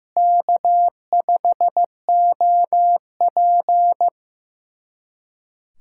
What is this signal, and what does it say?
Morse code, heard as a keyed tone, about 15 wpm, 700 Hz. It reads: K5OP
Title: USA Callsign DE Callsign - 4 Characters - 15wpm